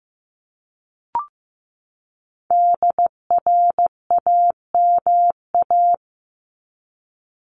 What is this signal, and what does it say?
Morse code, heard as a keyed tone, about 15 wpm, 700 Hz. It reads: DRAMA
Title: Top 1251-1500 Words 15wpm